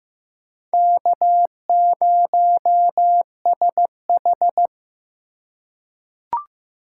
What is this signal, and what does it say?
Morse code, heard as a keyed tone, about 15 wpm, 700 Hz. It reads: K0SH
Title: USA Call Signs - 4x Character Spacing - 15wpm